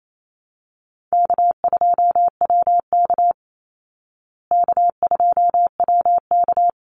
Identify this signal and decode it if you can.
Morse code, heard as a keyed tone, about 28 wpm, 700 Hz. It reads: K2WK K2WK
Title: Call Signs Repeated 3x - 28wpm